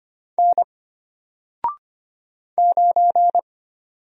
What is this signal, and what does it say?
Morse code, heard as a keyed tone, about 25 wpm, 700 Hz. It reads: N 9
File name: Single-Letter-Number-25wpm